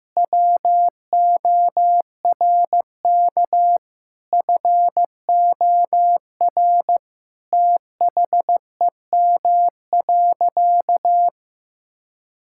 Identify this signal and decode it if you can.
Morse code, heard as a keyed tone, about 15 wpm, 700 Hz. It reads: WORK FOR THEM.
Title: Sentences from Top 1000 Words - Review Part 5 - 15wpm